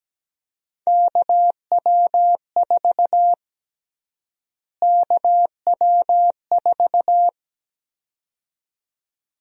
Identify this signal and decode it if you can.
Morse code, heard as a keyed tone, about 17 wpm, 700 Hz. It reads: KW4 KW4
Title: USA Call Sign Prefixes - Repeated 2x - 17wpm